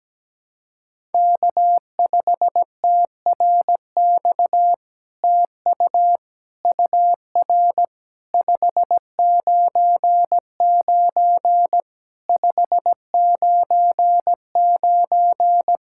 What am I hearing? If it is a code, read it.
Morse code, heard as a keyed tone, about 17 wpm, 700 Hz. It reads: K5TRX TU UR 599 599
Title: SOTA QSO Practice 17wpm